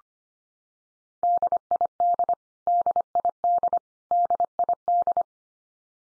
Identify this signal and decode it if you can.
Morse code, heard as a keyed tone, about 25 wpm, 700 Hz. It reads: DID DID DID
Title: Top 100 Words 25wpm x3